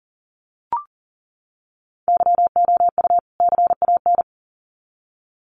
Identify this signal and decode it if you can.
Morse code, heard as a keyed tone, about 40 wpm, 700 Hz. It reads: YOU CAN
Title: Sets of 2 Words 40wpm